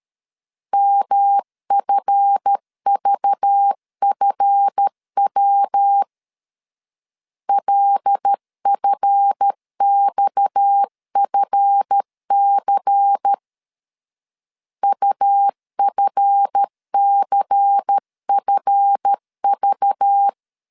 Morse code, heard as a keyed tone, about 13 wpm, 800 Hz. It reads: MFVFW LFXFC UFCFV
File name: LEZIONI / 7 GF9Z / 7 GF 12wpm 2122